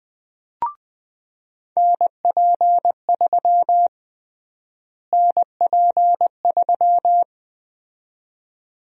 Morse code, heard as a keyed tone, about 20 wpm, 700 Hz. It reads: NP3 NP3
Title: USA Call Sign Prefixes - Repeated 2x - 20wpm